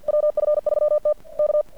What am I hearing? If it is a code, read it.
Morse code, heard as a keyed tone, about 50 wpm, 600 Hz. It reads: WF3T K
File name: archived / CRSnet